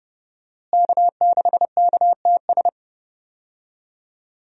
Morse code, heard as a keyed tone, about 32 wpm, 700 Hz. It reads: K6KTS
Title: USA Call Signs - 5 Characters - Repeated 2x - 30wpm